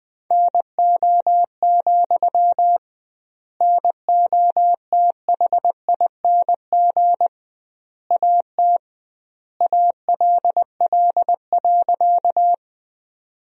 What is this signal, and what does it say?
Morse code, heard as a keyed tone, about 21 wpm, 700 Hz. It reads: NO, NOTHING AT ALL.
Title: 2x Word-Spacing - Sentences from Top 500 Words - 20wpm